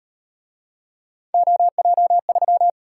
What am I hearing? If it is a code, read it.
Morse code, heard as a keyed tone, about 38 wpm, 700 Hz. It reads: OJ3